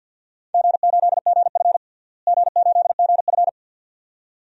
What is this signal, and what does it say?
Morse code, heard as a keyed tone, about 50 wpm, 700 Hz. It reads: G8GF G8GF
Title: DXCC-100 Call Signs - 4 Characters - Repeated 2x - 50wpm